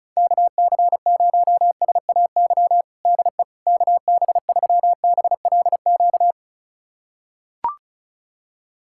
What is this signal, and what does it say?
Morse code, heard as a keyed tone, about 36 wpm, 690 Hz. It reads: KC0SAY DE KB3BLQ
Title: USA Callsign DE Callsign - 6 Characters - 35wpm